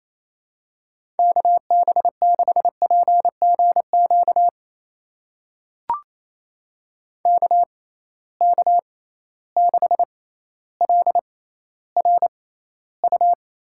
Morse code, heard as a keyed tone, about 28 wpm, spaced as for 14 wpm, 700 Hz. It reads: KB6PGQ KK6LRU
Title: USA Call Signs - 6 Characters - 6x Character Spacing - 28wpm